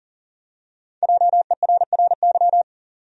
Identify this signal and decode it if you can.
Morse code, heard as a keyed tone, about 40 wpm, 700 Hz. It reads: JERRY
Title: USA Common Names - 40wpm